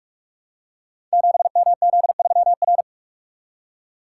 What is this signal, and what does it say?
Morse code, heard as a keyed tone, about 44 wpm, 700 Hz. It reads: 7MZ3R
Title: Code Groups - 5 Letter:Numbers - 45wpm